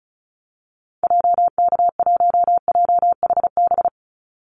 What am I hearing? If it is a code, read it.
Morse code, heard as a keyed tone, about 34 wpm, 700 Hz. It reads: JK1JHB
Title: Call Signs Repeated 2x - 35wpm